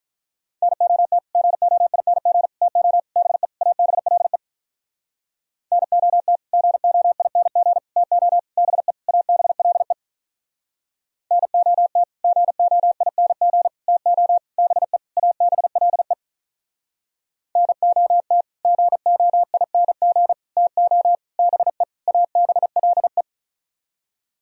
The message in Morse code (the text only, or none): NOT GOING TO BE ABLE NOT GOING TO BE ABLE NOT GOING TO BE ABLE NOT GOING TO BE ABLE